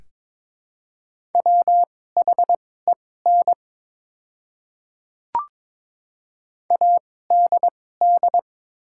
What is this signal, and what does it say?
Morse code, heard as a keyed tone, about 22 wpm, spaced as for 16 wpm, 700 Hz. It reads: WHEN ADD
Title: 2x Character Spacing - Top 200 Words - 22wpm